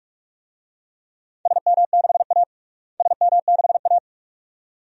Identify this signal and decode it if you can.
Morse code, heard as a keyed tone, about 46 wpm, 700 Hz. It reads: SM6A SM6A
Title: DXCC-100 Call Signs - 4 Characters - Repeated 2x - 45wpm